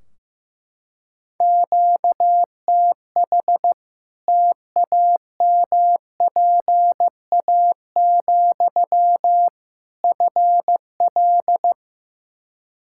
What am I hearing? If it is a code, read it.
Morse code, heard as a keyed tone, about 15 wpm, 700 Hz. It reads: QTH TAMPA, FL
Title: Practice QSO - Exchange 1 - Beginning - 15wpm